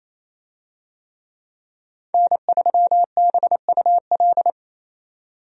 Morse code, heard as a keyed tone, about 28 wpm, 700 Hz. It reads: N3BUL
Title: USA Call Signs - 5 Characters - 28wpm